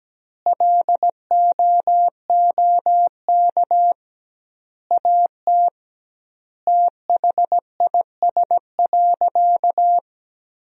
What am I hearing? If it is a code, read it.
Morse code, heard as a keyed tone, about 17 wpm, 700 Hz. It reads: LOOK AT THIS.
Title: 2x Word-Spacing - Sentences from Top 300 Words - 17wpm